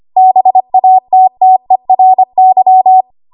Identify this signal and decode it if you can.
Morse code, heard as a keyed tone, about 25 wpm, 750 Hz. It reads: BATTERY
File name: Archive / 25 WPM / 25WPM